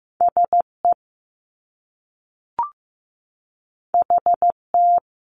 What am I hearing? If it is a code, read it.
Morse code, heard as a keyed tone, about 15 wpm, 700 Hz. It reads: SE HT
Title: Top Two Letter Combinations in Words 15wpm